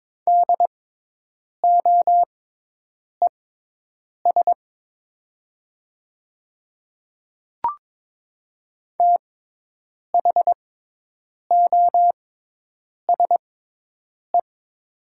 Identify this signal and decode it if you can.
Morse code, heard as a keyed tone, about 22 wpm, spaced as for 8 wpm, 700 Hz. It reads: DOES THOSE
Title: 6x Character Spacing - Top 500 Words - 22wpm